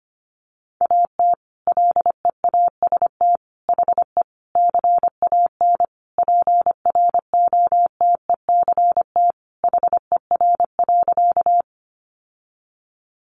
Morse code, heard as a keyed tone, about 25 wpm, 700 Hz. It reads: AT LEAST HE CAN PROTECT HER.
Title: Sentences from Top 900 Words 25wpm